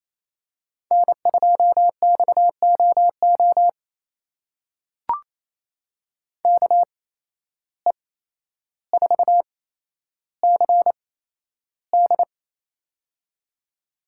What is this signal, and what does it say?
Morse code, heard as a keyed tone, about 28 wpm, spaced as for 12 wpm, 700 Hz. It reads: N2XOO KE4CD
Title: USA Call Signs - 5 Characters - 8x Character Spacing - 28wpm